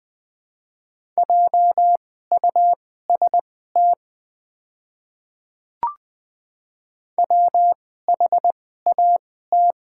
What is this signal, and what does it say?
Morse code, heard as a keyed tone, about 20 wpm, spaced as for 14 wpm, 690 Hz. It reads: JUST WHAT
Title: 2x Character Spacing - Top 200 Words - 20wpm